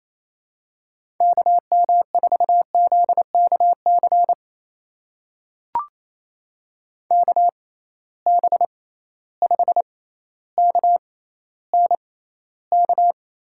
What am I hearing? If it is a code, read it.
Morse code, heard as a keyed tone, about 28 wpm, spaced as for 14 wpm, 700 Hz. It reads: KM4ZKC KB5KNK
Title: USA Call Signs - 6 Characters - 6x Character Spacing - 28wpm